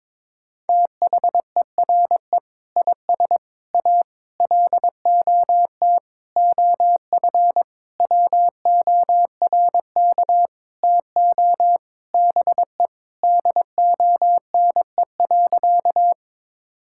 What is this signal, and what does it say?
Morse code, heard as a keyed tone, about 22 wpm, 700 Hz. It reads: THERE IS A LOT OF WORK TO BE DONE.
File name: Sentences with 7-Word Phrases - 22wpm